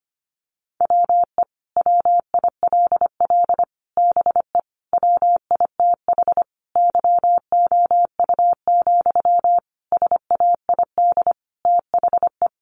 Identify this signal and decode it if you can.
Morse code, heard as a keyed tone, about 25 wpm, 700 Hz. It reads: WE WILL BE WITH YOU, SAID THE
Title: Sentences from Top 600 Words 25wpm